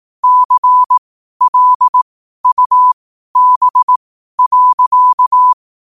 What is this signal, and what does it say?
Morse code, heard as a keyed tone, about 18 wpm, spaced as for 12 wpm, 1000 Hz. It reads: CLUB.
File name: qotdfiles / 12wpm